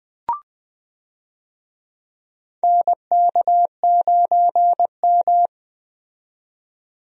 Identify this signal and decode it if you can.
Morse code, heard as a keyed tone, about 20 wpm, 700 Hz. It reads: NK9M
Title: USA Call Signs - 4 Characters - 20wpm